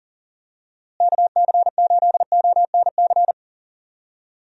Morse code, heard as a keyed tone, about 40 wpm, 700 Hz. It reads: KC8ONC
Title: USA Call Signs - 6 Characters - Repeated 3x - 40wpm